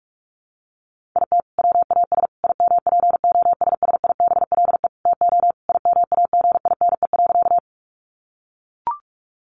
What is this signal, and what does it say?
Morse code, heard as a keyed tone, about 44 wpm, 700 Hz. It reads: IT WAS IMPOSSIBLE TO IMAGINE.
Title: Sentences from Top 1400 Words 45wpm